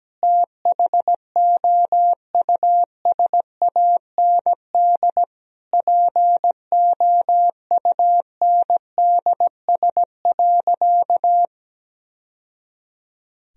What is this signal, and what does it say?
Morse code, heard as a keyed tone, about 17 wpm, 700 Hz. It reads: THOUSAND POUNDS.
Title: Sentences from Top 1000 Words - Review Part 3 - 17wpm